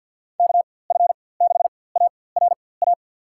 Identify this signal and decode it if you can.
Morse code, heard as a keyed tone, about 52 wpm, spaced as for 23 wpm, 700 Hz. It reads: KF6ARA
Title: USA Call Signs - 6 Characters - 4x Character Spacing - 50wpm